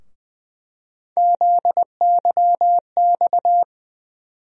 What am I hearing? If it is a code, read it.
Morse code, heard as a keyed tone, about 20 wpm, 700 Hz. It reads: ZYX
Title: USA Call Sign Postfixes - 3 Letters 20wpm